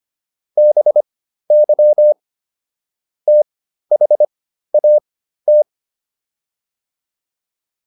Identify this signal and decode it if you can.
Morse code, heard as a keyed tone, about 25 wpm, spaced as for 13 wpm, 600 Hz. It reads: BY THAT